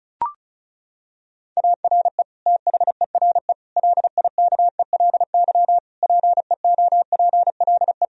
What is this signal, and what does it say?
Morse code, heard as a keyed tone, about 36 wpm, 700 Hz. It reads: ARE THERE LIKELY PEOPLE
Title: Sentences from Top 1000 Words - Review Part 4 - 35wpm